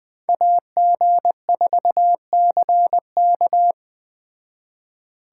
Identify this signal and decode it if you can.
Morse code, heard as a keyed tone, about 20 wpm, 700 Hz. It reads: AG4CK